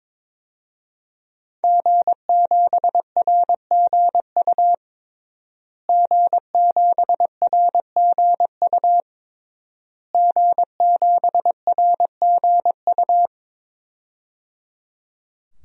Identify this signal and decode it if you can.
Morse code, heard as a keyed tone, about 22 wpm, 700 Hz. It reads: G7RGU G7RGU G7RGU
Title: DXCC-100 Call Signs - 5 Characters - Repeated 3x - 22wpm